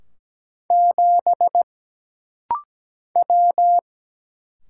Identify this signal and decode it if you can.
Morse code, heard as a keyed tone, about 17 wpm, 700 Hz. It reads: 7 W
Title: Single Letter-Number Alt - Rapid-Fire 17wpm